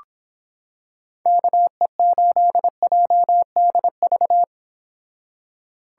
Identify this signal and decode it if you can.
Morse code, heard as a keyed tone, about 26 wpm, 700 Hz. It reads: KE8JDV